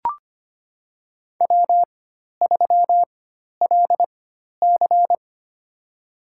Morse code, heard as a keyed tone, about 25 wpm, spaced as for 12 wpm, 700 Hz. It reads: W3LC
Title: USA Call Signs - 4x Character Spacing - 25wpm